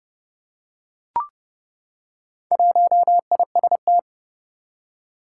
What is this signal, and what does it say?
Morse code, heard as a keyed tone, about 30 wpm, 700 Hz. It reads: 1IST
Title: Four Challenging Letter-Number Combinations 30wpm